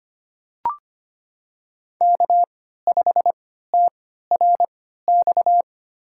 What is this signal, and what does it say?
Morse code, heard as a keyed tone, about 25 wpm, spaced as for 14 wpm, 700 Hz. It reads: K5TRX
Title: USA Call Signs - 3x Character Spacing - 25wpm